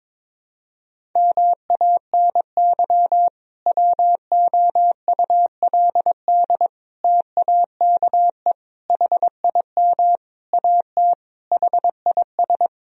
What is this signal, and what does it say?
Morse code, heard as a keyed tone, about 22 wpm, 700 Hz. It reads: MANY WOULD TAKE HIM AT HIS
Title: Sentences from Top 1000 Words - Review Part 1 - 22wpm